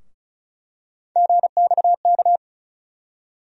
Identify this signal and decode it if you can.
Morse code, heard as a keyed tone, about 36 wpm, 700 Hz. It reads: GXK
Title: USA Call Sign Postfixes - 3 Letters 35wpm